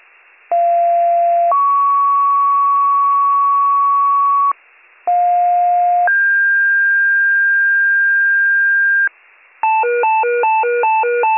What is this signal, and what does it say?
Two-tone page, heard as a one-second tone, then a three-second tone, about 690 then 1120 Hz, about 710 then 1600 Hz, with noise. It is heard from 0.5 to 9.1 s.